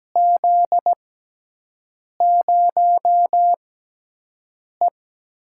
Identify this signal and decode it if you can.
Morse code, heard as a keyed tone, about 17 wpm, spaced as for 6 wpm, 700 Hz. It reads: Z0E
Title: USA Call Signs - 4 Characters - 6x Character Spacing - 17wpm